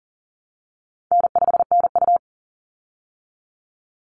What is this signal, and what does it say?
Morse code, heard as a keyed tone, about 40 wpm, 700 Hz. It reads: N5NU
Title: Call Signs Repeated 3x - 40wpm